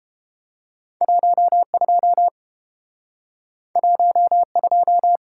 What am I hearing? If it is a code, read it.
Morse code, heard as a keyed tone, about 30 wpm, 700 Hz. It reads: 12 12